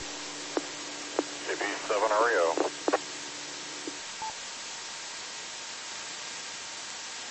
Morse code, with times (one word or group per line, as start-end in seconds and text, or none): none